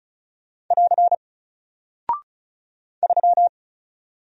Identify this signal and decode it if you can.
Morse code, heard as a keyed tone, about 36 wpm, 700 Hz. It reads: <AR> 3
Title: Single Character (Letter:Number:Punctuation:Prosign) Alt 35wpm